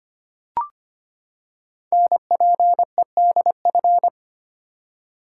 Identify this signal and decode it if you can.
Morse code, heard as a keyed tone, about 25 wpm, 700 Hz. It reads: NPEDF
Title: Code Groups - 5 Letter:Numbers - 25wpm